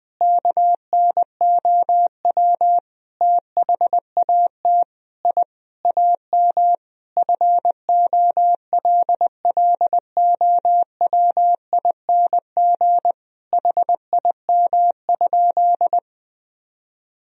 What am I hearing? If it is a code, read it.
Morse code, heard as a keyed tone, about 20 wpm, 700 Hz. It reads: KNOW THAT I AM FOLLOWING HIM?
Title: Sentences from Top 1000 Words - Review Part 4 - 20wpm